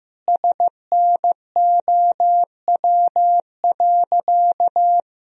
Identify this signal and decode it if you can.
Morse code, heard as a keyed tone, about 15 wpm, 700 Hz. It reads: SNOW.
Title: Sentences from Top 1000 Words - Review Part 3 - 15wpm